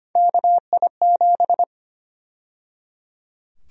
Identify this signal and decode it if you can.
Morse code, heard as a keyed tone, about 25 wpm, 700 Hz. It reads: KI7
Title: USA Call Sign Prefixes - 25wpm